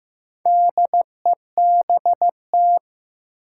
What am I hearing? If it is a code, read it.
Morse code, heard as a keyed tone, about 15 wpm, 700 Hz. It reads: DEBT
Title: Sentences from Top 1300 Words - New Word Drill - 15wpm